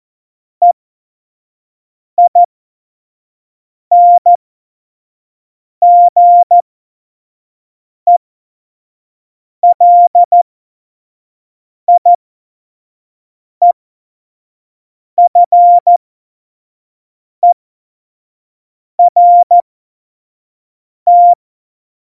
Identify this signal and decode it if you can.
Morse code, heard as a keyed tone, about 14 wpm, spaced as for 5 wpm, 700 Hz. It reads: EINGELIEFERT
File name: Morsekurs / Witze / Witz 29. September 2020, 25 F70